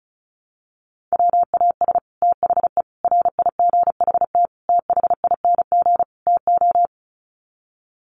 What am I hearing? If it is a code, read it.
Morse code, heard as a keyed tone, about 34 wpm, 700 Hz. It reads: WAS THE RIGHT THING TO
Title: Sets of 5 Words Encore2 35wpm